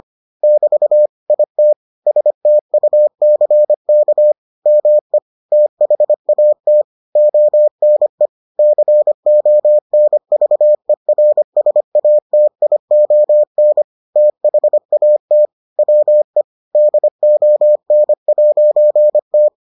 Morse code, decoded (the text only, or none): <BT> IT STUCK ME THAT ONE CONVERSATION THAT WE DON'T